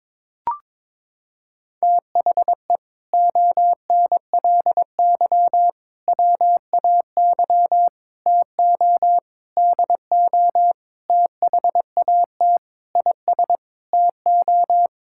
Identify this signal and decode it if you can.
Morse code, heard as a keyed tone, about 22 wpm, 700 Hz. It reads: THE ONLY WAY TO DO THAT IS TO
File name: practice / 22wpm / Sets of 8 Words - 22wpm